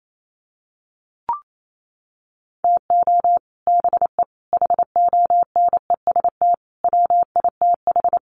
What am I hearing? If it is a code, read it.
Morse code, heard as a keyed tone, about 28 wpm, 700 Hz. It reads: TO BE HONEST WITH
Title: Sets of 5 Words Encore 28wpm